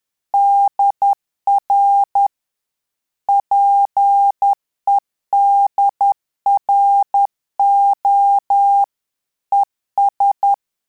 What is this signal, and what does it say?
Morse code, heard as a keyed tone, about 10 wpm, 800 Hz. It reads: DR PEDRO ES
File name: Telegraphy-group / QSO CW 10 PPM